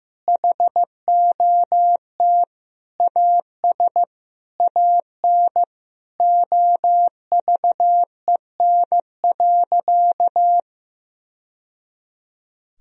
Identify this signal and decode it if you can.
Morse code, heard as a keyed tone, about 16 wpm, 700 Hz. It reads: HOT AS AN OVEN.
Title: Sentences from Top 1000 Words - Review Part 4 - 15wpm